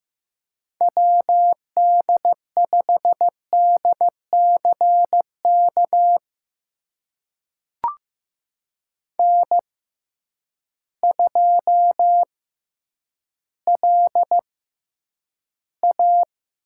Morse code, heard as a keyed tone, about 15 wpm, spaced as for 9 wpm, 700 Hz. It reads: WD5DCK N2LA
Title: USA Call Signs - 6x Character Spacing - 15wpm